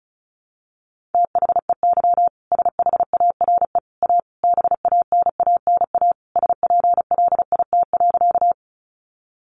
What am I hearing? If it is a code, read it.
Morse code, heard as a keyed tone, about 34 wpm, 700 Hz. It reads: THEY SHARE A BANANA SPLIT.
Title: Additional Easy Phrases 35wpm